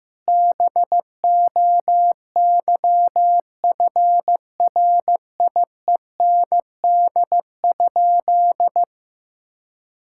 Morse code, heard as a keyed tone, about 15 wpm, 700 Hz. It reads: BOYFRIEND?
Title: Sentences from Top 1000 Words - Review Part 5 - 15wpm